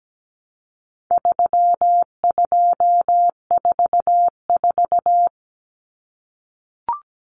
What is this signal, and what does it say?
Morse code, heard as a keyed tone, about 17 wpm, 700 Hz. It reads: 3244
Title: Common Cut Numbers - 4-Digit 17wpm